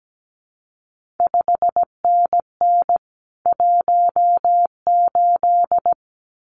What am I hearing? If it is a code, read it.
Morse code, heard as a keyed tone, about 17 wpm, 700 Hz. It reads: 5NN 18
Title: CQ WW DX Contest Exchange Encore - 17wpm